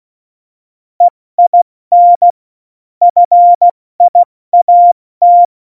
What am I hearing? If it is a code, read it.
Morse code, heard as a keyed tone, about 16 wpm, 700 Hz. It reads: EIN FIAT
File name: Morsekurs / Witze / Witz 04 August 2020, 70 F80